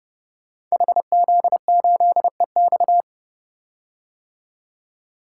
Morse code, heard as a keyed tone, about 28 wpm, 700 Hz. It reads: HZ8EX
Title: Code Groups - 5 Letter:Numbers - 30wpm